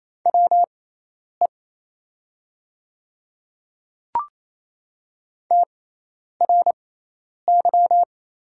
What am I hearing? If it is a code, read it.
Morse code, heard as a keyed tone, about 27 wpm, spaced as for 10 wpm, 700 Hz. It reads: WE TRY
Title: 6x Character Spacing - Top 200 Words - 28wpm